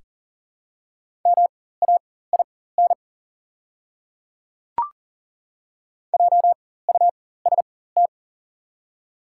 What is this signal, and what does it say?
Morse code, heard as a keyed tone, about 37 wpm, spaced as for 18 wpm, 700 Hz. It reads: MAIN JUST
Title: 4x Character Spacing - Top 500 Words - 40wpm